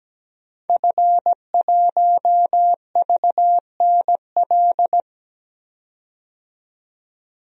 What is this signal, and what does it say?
Morse code, heard as a keyed tone, about 17 wpm, 700 Hz. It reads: F1VNL
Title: DXCC-100 Call Signs - 5 Characters - 17wpm